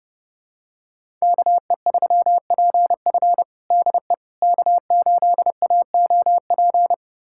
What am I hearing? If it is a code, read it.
Morse code, heard as a keyed tone, about 32 wpm, 700 Hz. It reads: KE3PF DE K8AOP
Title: USA Callsign DE Callsign - 5 Characters - 30wpm